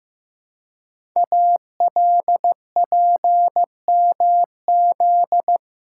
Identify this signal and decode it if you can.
Morse code, heard as a keyed tone, about 15 wpm, 700 Hz. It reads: ALPMZ
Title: Code Groups - 5 Letter:Numbers - 15wpm